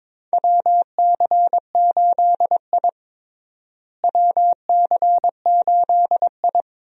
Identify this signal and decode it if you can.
Morse code, heard as a keyed tone, about 22 wpm, 700 Hz. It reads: WC8I WC8I
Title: USA Call Signs - 4 Characters - Repeated 2x - 22wpm